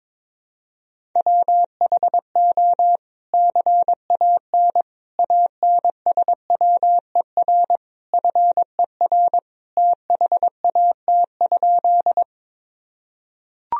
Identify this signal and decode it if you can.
Morse code, heard as a keyed tone, about 22 wpm, 700 Hz. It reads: WHO CAN ANSWER FER THAT?
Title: Sentences from Top 300 Words and Abbreviations 22wpm